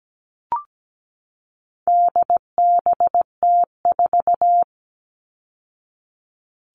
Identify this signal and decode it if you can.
Morse code, heard as a keyed tone, about 17 wpm, 700 Hz. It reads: DBT4
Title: All Cut Numbers - 4-Digit 17wpm